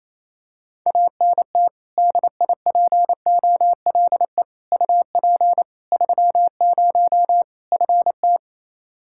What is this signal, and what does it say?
Morse code, heard as a keyed tone, about 28 wpm, 700 Hz. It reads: ANT DIPOLE UP 30 FT
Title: Practice QSO - Exchange 2 - Info - 28wpm